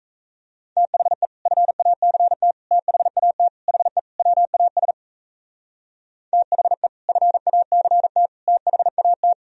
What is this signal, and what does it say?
Morse code, heard as a keyed tone, about 35 wpm, 700 Hz. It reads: THE FACT THAT HE WAS THE FACT THAT
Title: Speed-Racing - Sets of 5 Words 28wpm